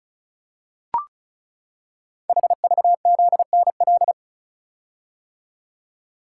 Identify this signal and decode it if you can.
Morse code, heard as a keyed tone, about 36 wpm, 700 Hz. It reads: HVZNL
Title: Code Groups - 5 Letters - 35wpm